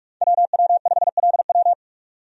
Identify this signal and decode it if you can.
Morse code, heard as a keyed tone, about 41 wpm, 700 Hz. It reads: WW5LW
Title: USA Call Signs - 5 Characters - 45wpm